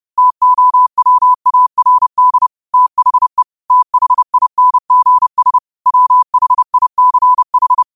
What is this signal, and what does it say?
Morse code, heard as a keyed tone, about 30 wpm, 1000 Hz. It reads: TOWARD THE THINGS WHICH
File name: qotdfiles / 30wpm